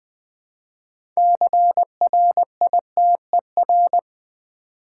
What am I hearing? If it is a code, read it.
Morse code, heard as a keyed tone, about 20 wpm, 700 Hz. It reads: CRITER
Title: Build A Word - 3 Letters At A Time - 9-Letter Words - 20wpm